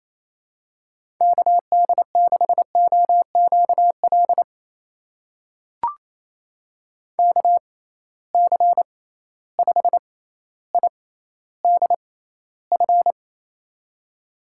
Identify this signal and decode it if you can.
Morse code, heard as a keyed tone, about 27 wpm, spaced as for 14 wpm, 700 Hz. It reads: KD6OQL KC5IDF
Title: USA Call Signs - 6 Characters - 6x Character Spacing - 28wpm